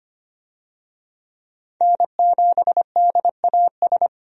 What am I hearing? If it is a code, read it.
Morse code, heard as a keyed tone, about 25 wpm, 700 Hz. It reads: N7DAS
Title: USA Call Signs - 5 Characters - 25wpm